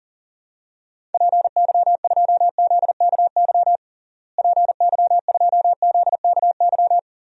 Morse code, heard as a keyed tone, about 40 wpm, 700 Hz. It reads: PY2ZKY PY2ZKY
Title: DXCC-100 Call Signs - 6 Characters - Repeated 3x - 40wpm